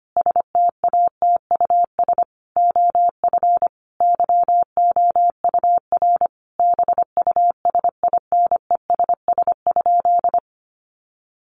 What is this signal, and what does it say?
Morse code, heard as a keyed tone, about 25 wpm, 700 Hz. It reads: STATUS OF YOUR BUSINESS?
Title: Sentences from Top 1400 Words 25wpm